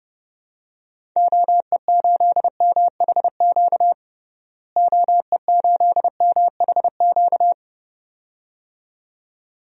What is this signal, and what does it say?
Morse code, heard as a keyed tone, about 30 wpm, 700 Hz. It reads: OE8MHQ OE8MHQ
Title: DXCC-100 Call Signs - 6 Characters - Repeated 2x - 30wpm